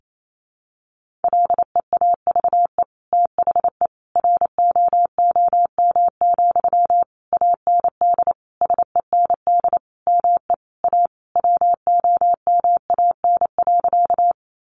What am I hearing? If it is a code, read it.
Morse code, heard as a keyed tone, about 28 wpm, 700 Hz. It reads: LEAVE THE ROOM, AND SEND ME A WOMAN.
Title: Sentences from Top 1500 Words 28wpm